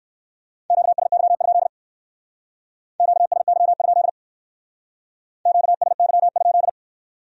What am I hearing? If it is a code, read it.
Morse code, heard as a keyed tone, about 67 wpm, 700 Hz. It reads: QSY? QSY? QSY?